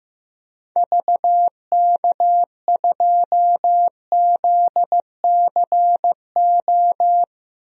Morse code, heard as a keyed tone, about 15 wpm, 700 Hz. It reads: VK2ZCO
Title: DXCC-100 Call Signs - 6 Characters - 15wpm